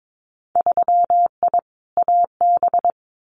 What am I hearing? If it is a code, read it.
Morse code, heard as a keyed tone, about 22 wpm, 700 Hz. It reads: 3I AB
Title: Winter Field Day Exchanges Encore 22wpm